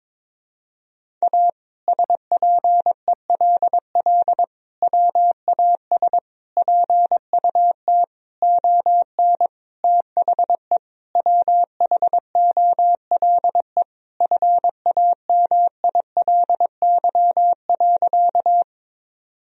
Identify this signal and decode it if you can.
Morse code, heard as a keyed tone, about 22 wpm, 700 Hz. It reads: A SPELL WAS PUT ON THE WHOLE FAMILY.
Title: Sentences from Top 1000 Words - Review Part 1 - 22wpm